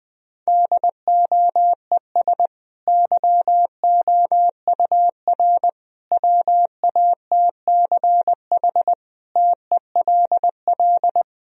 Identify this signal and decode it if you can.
Morse code, heard as a keyed tone, about 20 wpm, 700 Hz. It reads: DOES YOUR WATCH TELL
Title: Sentences from Top 1000 Words - Review Part 1 - 20wpm